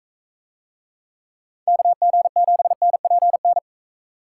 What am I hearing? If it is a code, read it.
Morse code, heard as a keyed tone, about 42 wpm, 700 Hz. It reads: KG7NPN